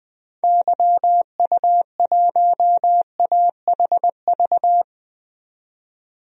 Morse code, heard as a keyed tone, about 20 wpm, 700 Hz. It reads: YU1AHV
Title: DXCC-100 Call Signs - 6 Characters - 20wpm